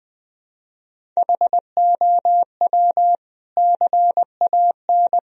Morse code, heard as a keyed tone, about 20 wpm, 700 Hz. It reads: HOW CAN
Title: Sentences from Top 1000 Words - Review Part 1 - 20wpm